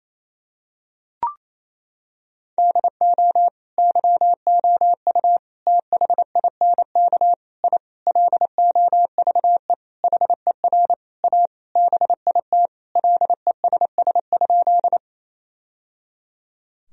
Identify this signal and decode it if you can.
Morse code, heard as a keyed tone, about 28 wpm, 700 Hz. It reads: DO YOU THINK I LOVE HER A BIT LESS?
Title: Sentences from Top 1000 Words - Review Part 2 - 28wpm